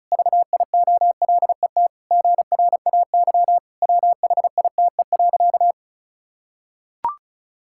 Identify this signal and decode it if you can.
Morse code, heard as a keyed tone, about 34 wpm, 700 Hz. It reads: VIOLET GRAY WHITE.
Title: Pangram Challenge v2 - 35wpm